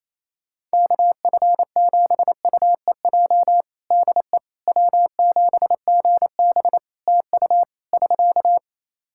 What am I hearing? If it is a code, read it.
Morse code, heard as a keyed tone, about 27 wpm, 700 Hz. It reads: KF7UEJ DE W7GB TU <SK>
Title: Practice QSO - Exchange 4 - Ending - 28wpm